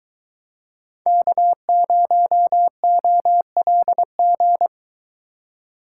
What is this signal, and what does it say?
Morse code, heard as a keyed tone, about 24 wpm, 700 Hz. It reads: K0OLG